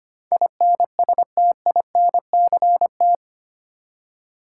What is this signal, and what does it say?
Morse code, heard as a keyed tone, about 25 wpm, 700 Hz. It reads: INSTINCT
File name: 8 Letter Words - First 3 Letters Given - 25wpm